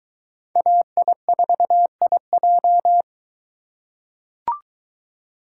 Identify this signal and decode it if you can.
Morse code, heard as a keyed tone, about 20 wpm, 700 Hz. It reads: AI4IJ